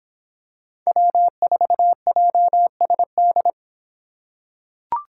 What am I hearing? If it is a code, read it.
Morse code, heard as a keyed tone, about 26 wpm, 700 Hz. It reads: W4JSD